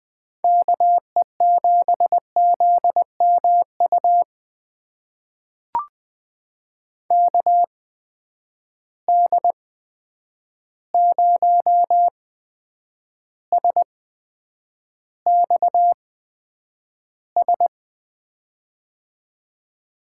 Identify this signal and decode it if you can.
Morse code, heard as a keyed tone, about 20 wpm, spaced as for 9 wpm, 700 Hz. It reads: KE7ZMU KD0SXS
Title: USA Call Signs - 6 Characters - 8x Character Spacing - 20wpm